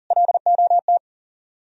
Morse code, heard as a keyed tone, about 54 wpm, spaced as for 47 wpm, 700 Hz. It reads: LOT
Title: Speed-Racing - Sets of 2 Words 40wpm